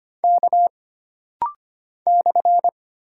Morse code, heard as a keyed tone, about 25 wpm, 700 Hz. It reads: K /
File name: Single Character (Letter:Number:Punctuation:Prosign) - Rapid-Fire - 25wpm